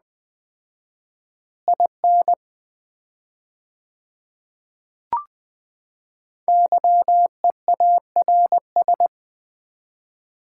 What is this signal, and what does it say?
Morse code, heard as a keyed tone, about 20 wpm, 700 Hz. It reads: IN YEARS
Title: 4x Word-Spacing - Sets of 2 Words - 20wpm